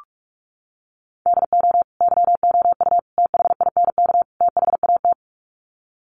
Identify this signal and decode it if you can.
Morse code, heard as a keyed tone, about 44 wpm, 700 Hz. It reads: DO YOU THINK THAT
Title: Sets of 4 Words 45wpm